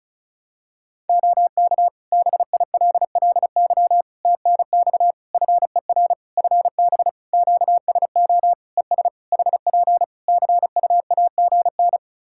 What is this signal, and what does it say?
Morse code, heard as a keyed tone, about 36 wpm, 700 Hz. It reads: OK BILLY TNX FER FB QSO ES HP CUAGN
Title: Practice QSO - Exchange 4 - Ending - 35wpm